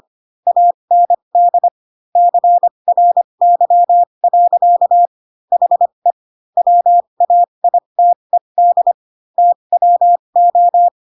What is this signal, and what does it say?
Morse code, heard as a keyed tone, about 25 wpm, spaced as for 22 wpm, 700 Hz. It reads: AND CRY. HE WAITED TWO